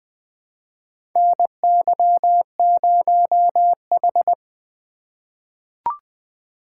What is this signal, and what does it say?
Morse code, heard as a keyed tone, about 20 wpm, 700 Hz. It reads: NY0H
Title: USA Call Signs - 4 Characters - 2x Character Spacing - 20wpm